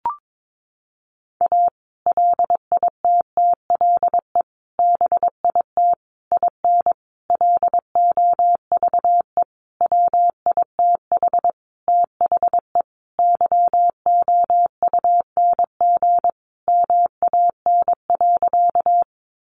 Morse code, heard as a keyed tone, about 22 wpm, 700 Hz. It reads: A LITTLE BIT IN LOVE WITH THE YOUNG MAN.
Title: Sentences from Top 900 Words 22wpm